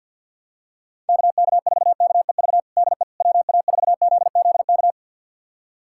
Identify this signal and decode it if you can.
Morse code, heard as a keyed tone, about 50 wpm, 700 Hz. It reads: KK4KEV DE WA4ZZK
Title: USA Callsign DE Callsign - 6 Characters - 50wpm